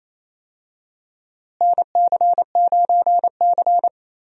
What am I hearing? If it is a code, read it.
Morse code, heard as a keyed tone, about 27 wpm, 700 Hz. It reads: NC9C
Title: USA Call Signs - 4 Characters - Repeated 3x - 28wpm